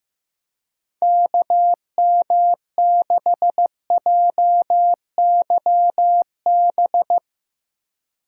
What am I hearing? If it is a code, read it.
Morse code, heard as a keyed tone, about 15 wpm, 700 Hz. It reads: KM6JYB
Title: Speed Racing - USA 2x3 Call Signs 15wpm